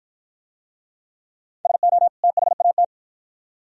Morse code, heard as a keyed tone, about 48 wpm, 700 Hz. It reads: SO THAT